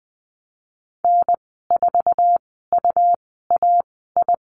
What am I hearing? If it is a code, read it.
Morse code, heard as a keyed tone, about 20 wpm, spaced as for 15 wpm, 700 Hz. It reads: N4UAI
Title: 128 Lesson 21g - Number 6 - Call Signs 20wpm